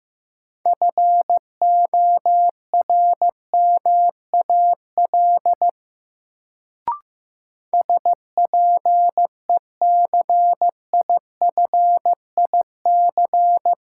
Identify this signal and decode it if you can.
Morse code, heard as a keyed tone, about 15 wpm, 700 Hz. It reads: FORMAL SPECIFIC
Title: Sentences from Top 1300 Words - New Word Drill Alt - 15wpm